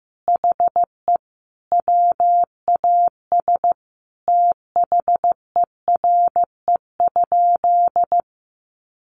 Morse code, heard as a keyed tone, about 15 wpm, 700 Hz. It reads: HE WAS THERE?
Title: Sentences from Top 1200 Words 15wpm